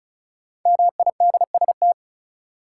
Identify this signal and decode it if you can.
Morse code, heard as a keyed tone, about 34 wpm, 700 Hz. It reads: MIDST
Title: Sets of 4 Words - New Word Drill - 35wpm